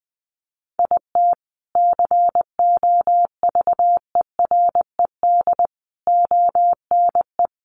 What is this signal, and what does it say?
Morse code, heard as a keyed tone, about 20 wpm, 700 Hz. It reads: IT COVERED ONE
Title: A Sound Of Thunder 20wpm